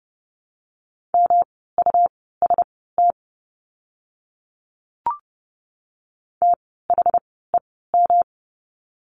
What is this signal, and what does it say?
Morse code, heard as a keyed tone, about 30 wpm, spaced as for 17 wpm, 700 Hz. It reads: MUST THEM
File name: course / 30:10wpm / 237 Lesson 38f - Letter q - Words 30wpm